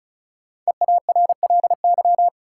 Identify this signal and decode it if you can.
Morse code, heard as a keyed tone, about 36 wpm, 700 Hz. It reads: EARLY
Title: Sentences from Top 400 Words - New Word Drill - 35wpm